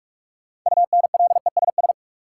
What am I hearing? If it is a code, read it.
Morse code, heard as a keyed tone, about 42 wpm, 700 Hz. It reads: UNLESS
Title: Sentences from Top 1000 Words - Review Part 5 - New Word Drill - 45wpm